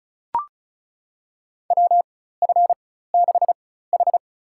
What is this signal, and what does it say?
Morse code, heard as a keyed tone, about 36 wpm, spaced as for 16 wpm, 700 Hz. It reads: WF6H
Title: USA Call Signs - 4 Characters - 4x Character Spacing - 35wpm